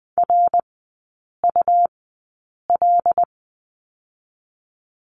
Morse code, heard as a keyed tone, about 20 wpm, spaced as for 8 wpm, 700 Hz. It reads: RUL
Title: 107 Lesson 18e - Letter m - Three Characters 20wpm